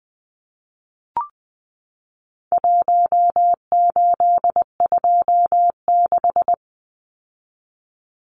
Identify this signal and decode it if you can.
Morse code, heard as a keyed tone, about 20 wpm, 700 Hz. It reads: 1826
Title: Numbers - Contest Style - 4-Digit 20wpm